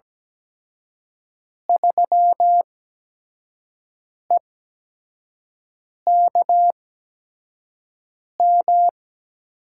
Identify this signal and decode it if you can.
Morse code, heard as a keyed tone, about 17 wpm, spaced as for 5 wpm, 700 Hz. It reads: 3EKM
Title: USA Call Signs - 6 Characters - 8x Character Spacing - 17wpm